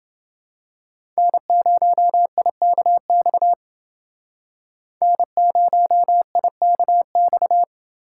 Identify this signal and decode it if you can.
Morse code, heard as a keyed tone, about 28 wpm, 700 Hz. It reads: N0IKX N0IKX